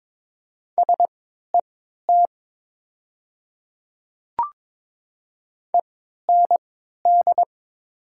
Morse code, heard as a keyed tone, about 21 wpm, spaced as for 12 wpm, 700 Hz. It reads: SET END
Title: 3x Character Spacing - Top 300 Words - 22wpm